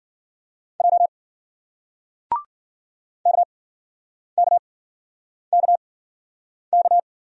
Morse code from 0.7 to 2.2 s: J

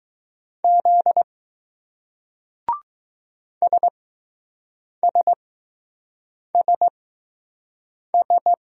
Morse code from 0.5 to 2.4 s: Z

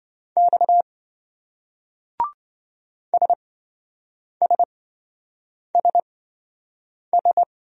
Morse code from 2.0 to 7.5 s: SSSS

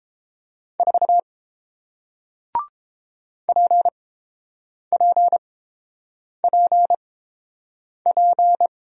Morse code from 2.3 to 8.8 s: PPPP

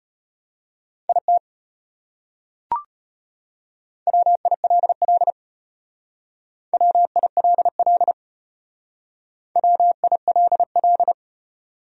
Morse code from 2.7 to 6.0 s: WILL

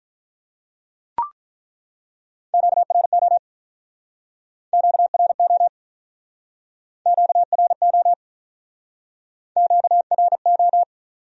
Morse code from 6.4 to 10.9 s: QRO QRO

